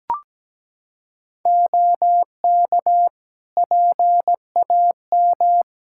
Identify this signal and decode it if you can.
Morse code, heard as a keyed tone, about 17 wpm, 700 Hz. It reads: OK PAM
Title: Practice QSO - Exchange 3 - More Info - 17wpm